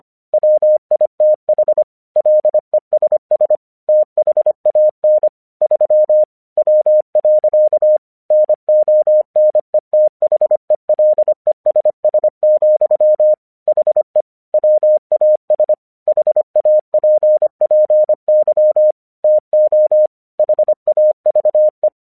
Morse code, heard as a keyed tone, about 25 wpm, 610 Hz. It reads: WITH LESS THAN 3 W. NONETHELESS, HE WAS HAPPY TO HAVE
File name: 1 English 25wpm / AQC26